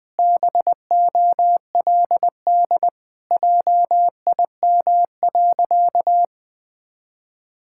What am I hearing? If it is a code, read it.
Morse code, heard as a keyed tone, about 20 wpm, 700 Hz. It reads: BOLD JIM.
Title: Pangram Challenge v2 - 20wpm